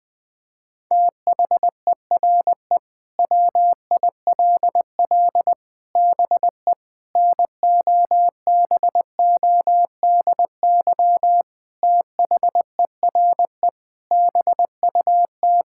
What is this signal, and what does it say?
Morse code, heard as a keyed tone, about 20 wpm, 700 Hz. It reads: THERE WILL BE NOBODY THERE BUT
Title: Sentences from Top 1600 Words 20wpm